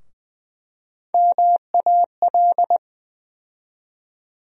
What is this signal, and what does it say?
Morse code, heard as a keyed tone, about 20 wpm, 700 Hz. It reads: MAL
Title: USA Call Sign Postfixes - 3 Letters 20wpm